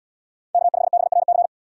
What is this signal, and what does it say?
Morse code, heard as a keyed tone, about 77 wpm, 700 Hz. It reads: K5BRC